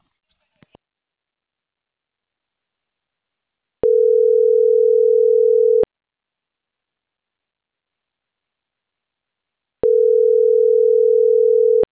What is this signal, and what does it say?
Ringback tone (North America), 440 + 480 Hz.